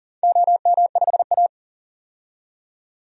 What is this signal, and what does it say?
Morse code, heard as a keyed tone, about 37 wpm, 700 Hz. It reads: OM5A
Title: DXCC-100 Call Signs - 4 Characters - 40wpm